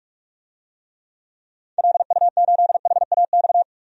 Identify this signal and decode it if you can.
Morse code, heard as a keyed tone, about 47 wpm, 700 Hz. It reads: LU8HAX
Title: DXCC-100 Call Signs - 6 Characters - 45wpm